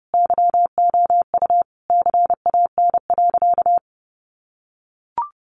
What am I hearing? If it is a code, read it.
Morse code, heard as a keyed tone, about 30 wpm, 700 Hz. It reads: YOU CAN.
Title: Sentences from Top 1200 Words 30wpm